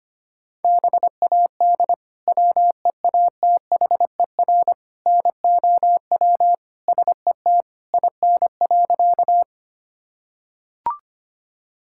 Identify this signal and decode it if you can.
Morse code, heard as a keyed tone, about 25 wpm, 700 Hz. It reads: BAD WEATHER NOW SET IN.
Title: Sentences from Top 1000 Words - Review Part 5 - 25wpm